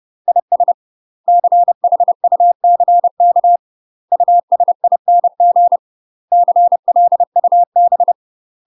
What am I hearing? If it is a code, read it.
Morse code, heard as a keyed tone, about 30 wpm, 700 Hz. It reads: IS CHUCK USING CLUB
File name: UK MASTER QSOs TRAINING / 30WPM / UK MASTER QSOs 0051